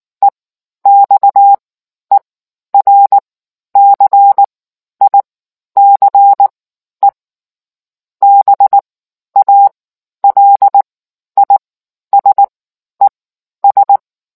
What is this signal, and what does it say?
Morse code, heard as a keyed tone, about 19 wpm, spaced as for 11 wpm, 800 Hz. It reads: EXERCICE BALISES